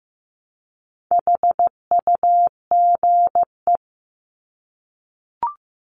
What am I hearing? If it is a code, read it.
Morse code, heard as a keyed tone, about 15 wpm, 700 Hz. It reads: HUGE
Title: Top 1001-1250 Words 15wpm